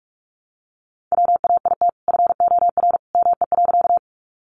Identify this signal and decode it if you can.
Morse code, heard as a keyed tone, about 44 wpm, 700 Hz. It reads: WAIT FOR ME.
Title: Sentences from Top 500 Words 45wpm